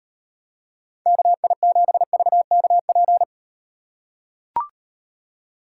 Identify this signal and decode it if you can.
Morse code, heard as a keyed tone, about 38 wpm, 700 Hz. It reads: KI7VKP